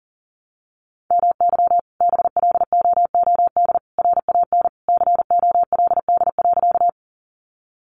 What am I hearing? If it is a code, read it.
Morse code, heard as a keyed tone, about 40 wpm, 700 Hz. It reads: MY BLOOD RAN COLD.
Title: Sentences from Top 1100 Words 40wpm